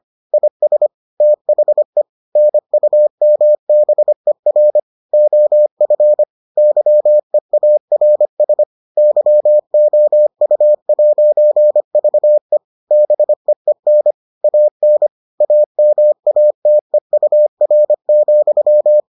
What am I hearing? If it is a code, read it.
Morse code, heard as a keyed tone, about 25 wpm, 600 Hz. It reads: IS THE NUMBER OF YEARS YOU'VE BEEN AN AMATEUR,